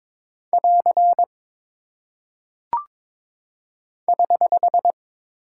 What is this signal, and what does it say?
Morse code, heard as a keyed tone, about 22 wpm, 700 Hz. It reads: <AR> <HH>
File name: practice / 22wpm / Single Punctuation-Prosign - 22wpm